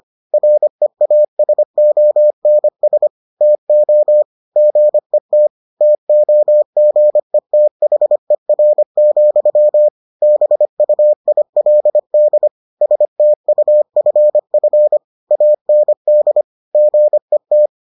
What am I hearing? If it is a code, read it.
Morse code, heard as a keyed tone, about 25 wpm, 600 Hz. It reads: REASONS TO GET TOGETHER, BUILD STUFF AND GET